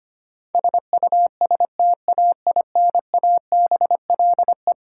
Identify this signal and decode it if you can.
Morse code, heard as a keyed tone, about 25 wpm, 700 Hz. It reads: SUSTAINABLE
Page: morse-code-ninja-practice-25wpm